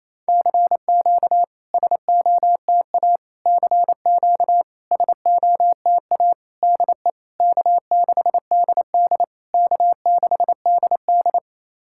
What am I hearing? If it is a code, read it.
Morse code, heard as a keyed tone, about 27 wpm, 690 Hz. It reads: CQ SOTA CQ SOTA DE K6DD K6DD
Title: SOTA QSO Practice 28wpm